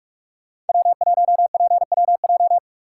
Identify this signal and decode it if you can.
Morse code, heard as a keyed tone, about 41 wpm, 700 Hz. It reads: W1PWJ
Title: USA Call Signs - 5 Characters - 2x Character Spacing - 45wpm